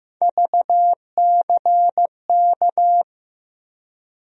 Morse code, heard as a keyed tone, about 15 wpm, 700 Hz. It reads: VCK
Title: USA Call Sign Postfixes - 3 Letters 15wpm